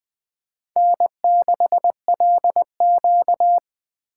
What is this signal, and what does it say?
Morse code, heard as a keyed tone, about 20 wpm, 700 Hz. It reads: N6LQ
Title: USA Call Signs - 4 Characters - 3x Character Spacing - 20wpm